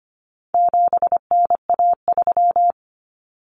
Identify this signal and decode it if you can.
Morse code, heard as a keyed tone, about 25 wpm, 700 Hz. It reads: 7NA3
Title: Common Cut Numbers - 4-Digit 25wpm